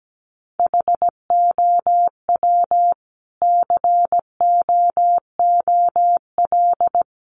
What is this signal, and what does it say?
Morse code, heard as a keyed tone, about 17 wpm, 700 Hz. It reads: HOW COOL
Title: Sentences from Top 1200 Words 17wpm